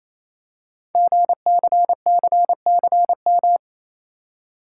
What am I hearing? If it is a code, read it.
Morse code, heard as a keyed tone, about 28 wpm, 690 Hz. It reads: GCCCM
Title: Code Groups - 5 Letters - 28wpm